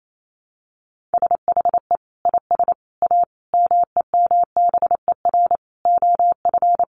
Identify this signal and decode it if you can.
Morse code, heard as a keyed tone, about 28 wpm, 700 Hz. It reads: SHE IS A MEMBER OF
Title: Sentences from Top 1100 Words 28wpm